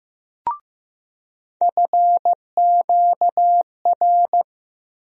Morse code, heard as a keyed tone, about 15 wpm, 700 Hz. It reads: FQR
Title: USA Call Sign Postfixes - 3 Letters 15wpm